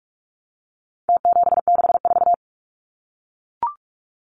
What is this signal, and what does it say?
Morse code, heard as a keyed tone, about 44 wpm, 700 Hz. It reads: T764
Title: Numbers - Contest Style - 4-Digit 45wpm